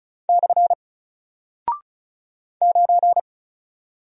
Morse code, heard as a keyed tone, about 36 wpm, 700 Hz. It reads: / 9
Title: Single Character (Letter:Number:Punctuation:Prosign) Alt 35wpm